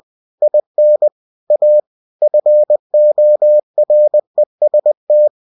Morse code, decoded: IN A FOREST

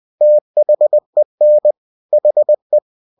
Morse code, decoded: THEN HE

